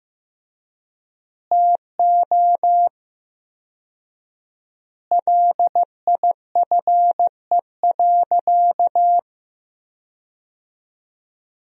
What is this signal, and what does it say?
Morse code, heard as a keyed tone, about 15 wpm, 700 Hz. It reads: TO LIFE.
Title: 4x Word-Spacing - Sentences from Top 400 Words - 15wpm